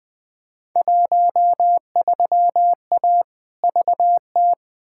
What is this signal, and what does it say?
Morse code, heard as a keyed tone, about 20 wpm, 700 Hz. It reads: 13A VT
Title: Field Day Exchanges Encore - 20wpm